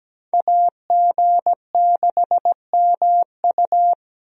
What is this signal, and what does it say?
Morse code, heard as a keyed tone, about 17 wpm, 700 Hz. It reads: AG6MU
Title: USA Call Signs - 5 Characters - 17wpm